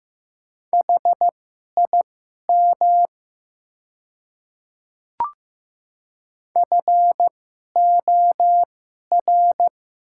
Morse code, heard as a keyed tone, about 15 wpm, spaced as for 11 wpm, 700 Hz. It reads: HIM FOR
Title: 2x Character Spacing - Top 100 Words - 15wpm